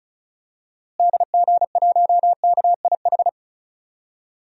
Morse code, heard as a keyed tone, about 36 wpm, 700 Hz. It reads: DG1KIH
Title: DXCC-100 Call Signs - 6 Characters - Repeated 3x - 35wpm